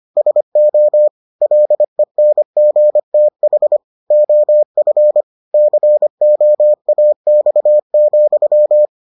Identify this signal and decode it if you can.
Morse code, heard as a keyed tone, about 25 wpm, 600 Hz. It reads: SO LENGTH OF COAX,